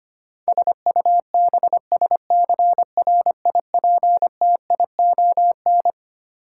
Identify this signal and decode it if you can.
Morse code, heard as a keyed tone, about 25 wpm, 700 Hz. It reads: SUBSCRIPTION
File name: Twelve Letter Words - 25wpm